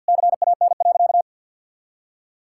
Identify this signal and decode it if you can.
Morse code, heard as a keyed tone, about 52 wpm, 690 Hz. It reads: CAN.
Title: Sentences from Top 1000 Words - Review Part 5 - 50wpm